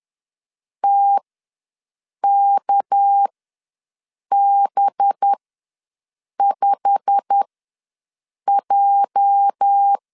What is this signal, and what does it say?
Morse code, heard as a keyed tone, about 10 wpm, spaced as for 6 wpm, 800 Hz. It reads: TKB5J
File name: LEZIONI / 10 SIGNS / 10A SLASH ws 10wpm 4121